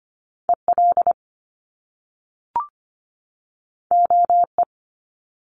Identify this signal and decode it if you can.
Morse code, heard as a keyed tone, about 25 wpm, 700 Hz. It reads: EL OE
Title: Top Two Letter Combinations in Words 25wpm